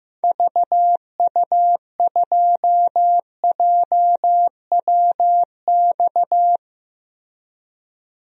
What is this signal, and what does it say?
Morse code, heard as a keyed tone, about 15 wpm, 700 Hz. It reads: VU2JWX
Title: DXCC-100 Call Signs - 6 Characters - Repeated 2x - 15wpm